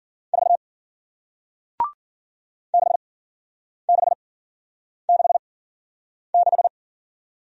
Morse code, heard as a keyed tone, about 54 wpm, spaced as for 9 wpm, 700 Hz. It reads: 4 6666